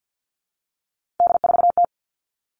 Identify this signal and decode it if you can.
Morse code, heard as a keyed tone, about 50 wpm, 700 Hz. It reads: D4T
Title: All Cut Numbers - 3-Digit 50wpm